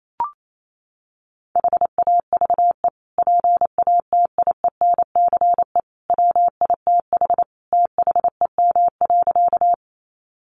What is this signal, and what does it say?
Morse code, heard as a keyed tone, about 28 wpm, 700 Hz. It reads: HAVE PATIENCE WITH THEM.
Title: Sentences from Top 1300 Words 28wpm